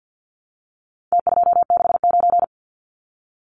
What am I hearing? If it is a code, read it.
Morse code, heard as a keyed tone, about 50 wpm, 700 Hz. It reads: T269
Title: Numbers - Contest Style - 4-Digit 50wpm